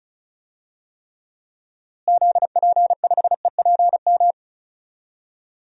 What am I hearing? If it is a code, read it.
Morse code, heard as a keyed tone, about 34 wpm, 700 Hz. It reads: ZP5EPM
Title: DXCC-100 Call Signs - 6 Characters - Repeated 2x - 35wpm